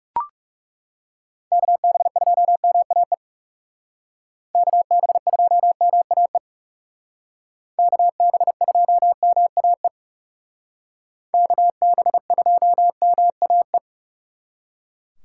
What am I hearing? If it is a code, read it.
Morse code, heard as a keyed tone, about 36 wpm, 700 Hz. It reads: KB2MAE KB2MAE KB2MAE KB2MAE
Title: Speed Racing - USA 2x3 Call Signs 30wpm